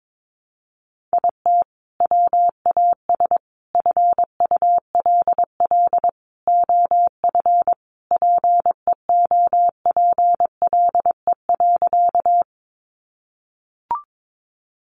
Morse code, heard as a keyed tone, about 22 wpm, 700 Hz. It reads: IT WAS FULL OF PEOPLE.
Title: Sentences from Top 500 Words 22wpm